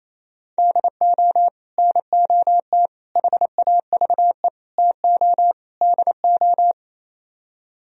Morse code, heard as a keyed tone, about 26 wpm, 700 Hz. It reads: DO NOT HAVE TO DO
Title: Speed-Racing - Sets of 5 Words 28wpm